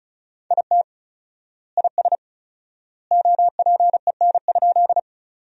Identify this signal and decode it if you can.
Morse code, heard as a keyed tone, about 34 wpm, 700 Hz. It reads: IT IS OPEN?
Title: 4x Word-Spacing - Sentences from Top 300 Words - 35wpm